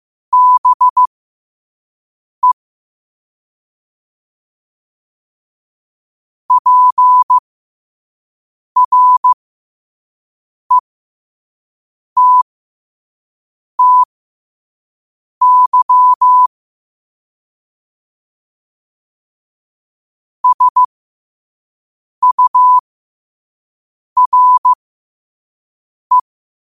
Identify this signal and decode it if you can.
Morse code, heard as a keyed tone, about 15 wpm, spaced as for 5 wpm, 1000 Hz. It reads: BE PRETTY SURE
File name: qotdfiles / 5wpm